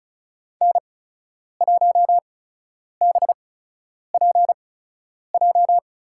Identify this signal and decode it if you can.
Morse code, heard as a keyed tone, about 34 wpm, spaced as for 10 wpm, 700 Hz. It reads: N1BPJ
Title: USA Call Signs - 5 Characters - 8x Character Spacing - 35wpm